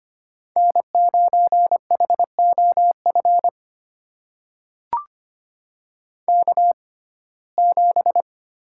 Morse code, heard as a keyed tone, about 25 wpm, spaced as for 18 wpm, 700 Hz. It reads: N9HOF K7
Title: USA Call Signs - 6x Character Spacing - 25wpm